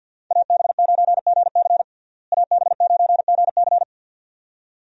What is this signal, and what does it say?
Morse code, heard as a keyed tone, about 52 wpm, 700 Hz. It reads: AB9GC AB9GC
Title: USA Call Signs - 5 Characters - Repeated 2x - 50wpm